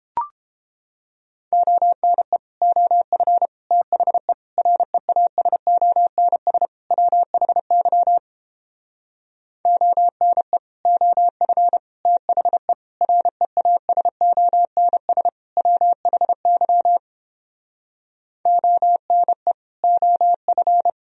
Morse code, heard as a keyed tone, about 29 wpm, 700 Hz. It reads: ONE OF THE REASONS WHY ONE OF THE REASONS WHY ONE OF